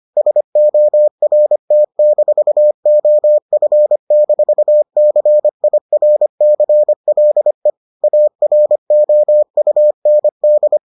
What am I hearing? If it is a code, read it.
Morse code, heard as a keyed tone, about 25 wpm, 600 Hz. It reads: SORT-OF-CIRCLE AROUND